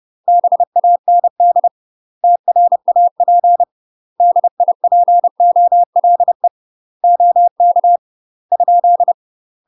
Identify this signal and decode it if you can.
Morse code, heard as a keyed tone, about 30 wpm, 700 Hz. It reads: BAND TRAP DIPOLE OK ?
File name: UK MASTER QSOs TRAINING / 30WPM / UK MASTER QSOs 0029